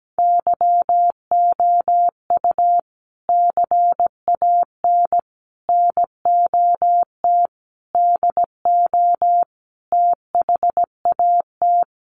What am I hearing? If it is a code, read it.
Morse code, heard as a keyed tone, about 17 wpm, 700 Hz. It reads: YOU CAN NOT DO THAT
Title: Sets of 5 Words 17wpm